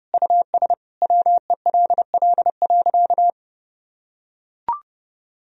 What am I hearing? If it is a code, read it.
Morse code, heard as a keyed tone, about 32 wpm, 700 Hz. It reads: US WELL.
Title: Sentences from Top 1000 Words - Review Part 4 - 30wpm